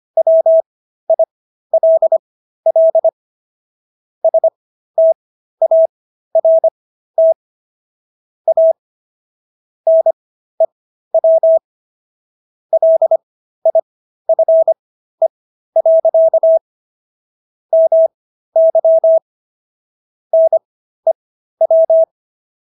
Morse code, heard as a keyed tone, about 25 wpm, spaced as for 13 wpm, 650 Hz. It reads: WILL START A NEW LIFE. MY NEW